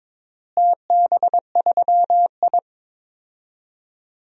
Morse code, heard as a keyed tone, about 22 wpm, 700 Hz. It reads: TB3I
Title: DXCC-100 Call Signs - 4 Characters - Repeated 3x - 22wpm